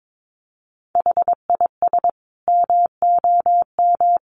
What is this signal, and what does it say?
Morse code, heard as a keyed tone, about 22 wpm, 700 Hz. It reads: HIS MOM
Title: Additional Easy Phrases 22wpm